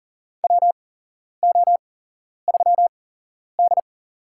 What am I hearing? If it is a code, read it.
Morse code, heard as a keyed tone, about 37 wpm, spaced as for 11 wpm, 700 Hz. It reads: WO3D
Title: USA Call Signs - 4 Characters - 8x Character Spacing - 40wpm